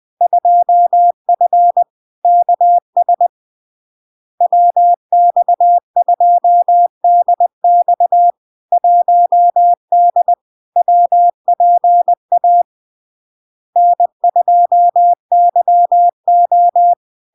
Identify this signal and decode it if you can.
Morse code, heard as a keyed tone, about 20 wpm, 700 Hz. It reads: 2F KS WX2DX 1D WPA N2YO
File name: LARK-FD-2022-Log-in-CW-20wpm-700-Hz